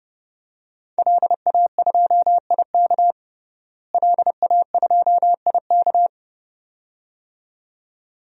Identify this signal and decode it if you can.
Morse code, heard as a keyed tone, about 32 wpm, 700 Hz. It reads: LA2IK LA2IK
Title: DXCC-100 Call Signs - 5 Characters - Repeated 2x - 30wpm